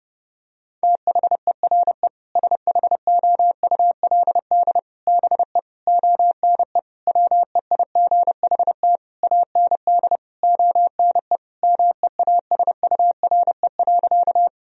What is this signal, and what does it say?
Morse code, heard as a keyed tone, about 30 wpm, 700 Hz. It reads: THERE SHOULD BE ONE WEIGHT AND ONE MEASURE.
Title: Sentences from Top 1000 Words - Review Part 3 - 30wpm